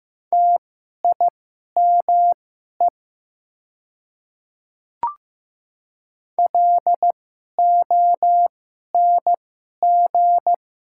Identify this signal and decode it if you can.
Morse code, heard as a keyed tone, about 15 wpm, spaced as for 11 wpm, 700 Hz. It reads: TIME LONG
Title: 2x Character Spacing - Top 200 Words - 15wpm